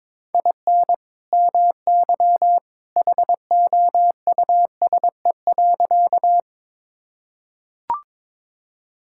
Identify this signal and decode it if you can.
Morse code, heard as a keyed tone, about 22 wpm, 700 Hz. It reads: IN MY HOUSE.
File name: practice / 22wpm / Sentences with 2-Word Phrases - 22wpm